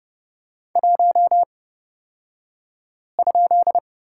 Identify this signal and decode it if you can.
Morse code, heard as a keyed tone, about 30 wpm, spaced as for 5 wpm, 700 Hz. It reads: 1?
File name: Lesson 36d - BT - Two Characters 30wpm